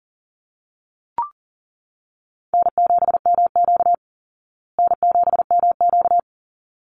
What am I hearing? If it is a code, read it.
Morse code, heard as a keyed tone, about 40 wpm, 700 Hz. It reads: N7MQ N7MQ
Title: Call Signs Repeated 3x - 40wpm